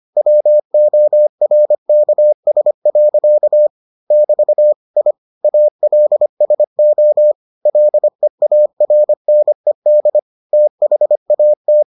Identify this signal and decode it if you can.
Morse code, heard as a keyed tone, about 25 wpm, 600 Hz. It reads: WORKS. <BT> I ALSO LEARNED THAT